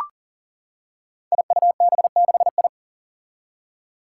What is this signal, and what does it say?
Morse code, heard as a keyed tone, about 38 wpm, 690 Hz. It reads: IUB6I
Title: Five Challenging Letter-Number Combinations 40wpm